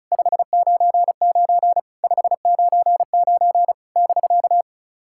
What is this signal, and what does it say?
Morse code, heard as a keyed tone, about 36 wpm, 660 Hz. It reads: 599 599 <BK>
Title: SOTA QSO Practice 35wpm